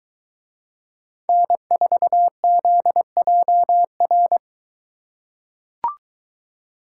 Morse code, heard as a keyed tone, about 23 wpm, 700 Hz. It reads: N4ZJR